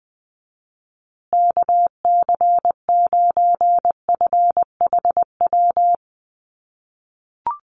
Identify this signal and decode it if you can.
Morse code, heard as a keyed tone, about 20 wpm, 700 Hz. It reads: KC9FHW
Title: USA Call Signs Repeated 3x 20wpm